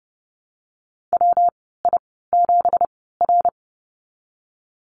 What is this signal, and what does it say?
Morse code, heard as a keyed tone, about 30 wpm, spaced as for 17 wpm, 700 Hz. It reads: WI7R
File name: Lesson 37g - Letter x - Call Signs 30wpm